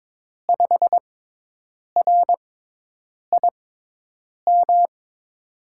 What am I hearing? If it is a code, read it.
Morse code, heard as a keyed tone, about 22 wpm, spaced as for 8 wpm, 700 Hz. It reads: 5RIM
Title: USA Call Signs - 6 Characters - 6x Character Spacing - 22wpm